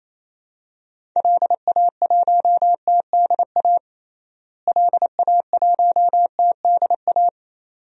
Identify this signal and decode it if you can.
Morse code, heard as a keyed tone, about 28 wpm, 700 Hz. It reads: LA1TDA LA1TDA
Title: DXCC-100 Call Signs - 6 Characters - Repeated 2x - 28wpm